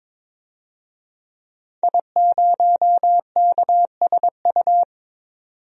Morse code, heard as a keyed tone, about 22 wpm, 700 Hz. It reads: I0KSU
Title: DXCC-100 Call Signs - 5 Characters - 22wpm